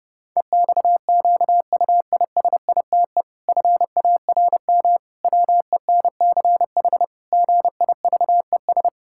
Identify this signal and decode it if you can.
Morse code, heard as a keyed tone, about 30 wpm, 700 Hz. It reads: EXQUISITE FARM WENCH GIVES
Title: Pangram Challenge v2 - 30wpm